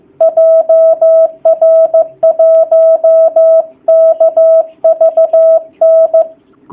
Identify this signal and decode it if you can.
Morse code, heard as a keyed tone, about 15 wpm, 660 Hz. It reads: JR1KVN
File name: jh8cbh / 2019cw / sp15- w